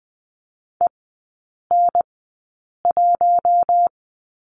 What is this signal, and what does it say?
Morse code, heard as a keyed tone, about 20 wpm, spaced as for 8 wpm, 700 Hz. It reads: EN1
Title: 053 Lesson 10e - Letter r - Three Characters 20wpm